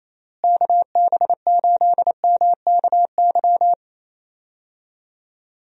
Morse code, heard as a keyed tone, about 28 wpm, 700 Hz. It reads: KB8MKY
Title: USA Call Signs - 6 Characters - Repeated 3x - 28wpm